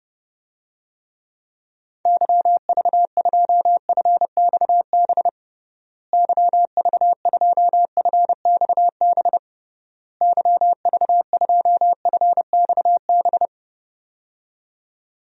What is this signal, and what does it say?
Morse code, heard as a keyed tone, about 32 wpm, 700 Hz. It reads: YV2FXB YV2FXB YV2FXB
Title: DXCC-100 Call Signs - 6 Characters - Repeated 3x - 30wpm